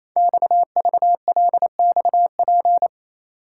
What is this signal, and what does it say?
Morse code, heard as a keyed tone, about 29 wpm, 700 Hz. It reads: XVLXP
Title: Code Groups - 5 Letters - 28wpm